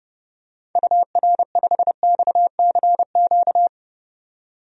Morse code, heard as a keyed tone, about 30 wpm, 690 Hz. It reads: UR5XCQ
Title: DXCC-100 Call Signs - 6 Characters - 30wpm